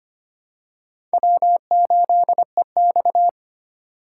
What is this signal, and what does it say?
Morse code, heard as a keyed tone, about 25 wpm, 700 Hz. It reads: W8EX
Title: USA Call Signs - 4 Characters - 4x Character Spacing - 25wpm